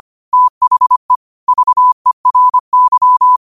Morse code, heard as a keyed tone, about 25 wpm, 1000 Hz. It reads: THE VERY
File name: qotdfiles / 25wpm